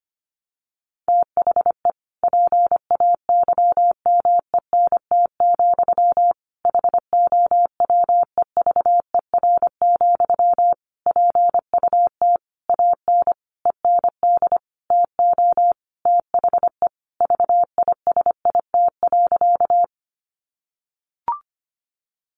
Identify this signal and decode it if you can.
Morse code, heard as a keyed tone, about 25 wpm, 700 Hz. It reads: THE PAYMENT, HOWEVER, PUT AN END TO THE VISIT.
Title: Sentences from Top 1200 Words 25wpm